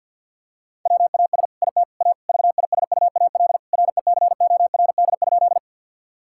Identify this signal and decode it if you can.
Morse code, heard as a keyed tone, about 50 wpm, 700 Hz. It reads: WAS IT A VISUAL RECORD?
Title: Sentences from Top 1000 Words - Review Part 6 - 50wpm